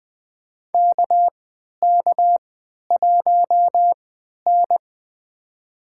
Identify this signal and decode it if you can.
Morse code, heard as a keyed tone, about 20 wpm, spaced as for 11 wpm, 700 Hz. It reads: KK1N
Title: USA Call Signs - 4 Characters - 3x Character Spacing - 20wpm